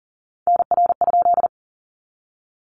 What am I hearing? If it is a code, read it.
Morse code, heard as a keyed tone, about 38 wpm, 700 Hz. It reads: NR?
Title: Common QSO Phrases 40wpm